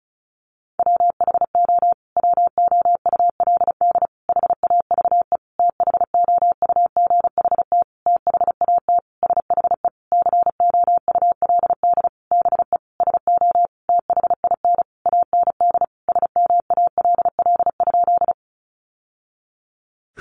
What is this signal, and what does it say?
Morse code, heard as a keyed tone, about 34 wpm, 700 Hz. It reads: WHO WOULD HAVE THOUGHT THAT SHE COULD BE SO THIN AND SMALL?
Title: Sentences from Top 1200 Words 35wpm